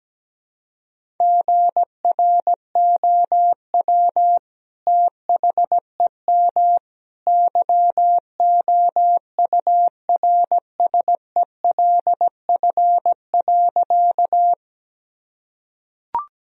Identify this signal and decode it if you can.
Morse code, heard as a keyed tone, about 17 wpm, 700 Hz. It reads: GROW THEM YOURSELF.
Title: Sentences from Top 1000 Words - Review Part 1 - 17wpm